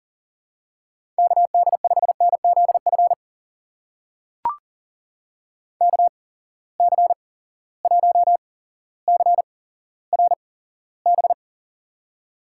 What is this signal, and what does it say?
Morse code, heard as a keyed tone, about 40 wpm, spaced as for 17 wpm, 700 Hz. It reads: KD5NZF KC1CRB
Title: USA Call Signs - 6 Characters - 8x Character Spacing - 40wpm